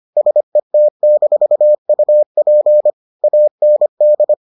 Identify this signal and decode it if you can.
Morse code, heard as a keyed tone, about 25 wpm, 600 Hz. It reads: SET-UP AND